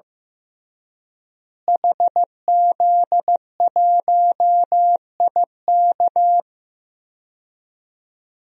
Morse code, heard as a keyed tone, about 15 wpm, 700 Hz. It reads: HZ1IK
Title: DXCC-100 Call Signs - 5 Characters - Repeated 3x - 15wpm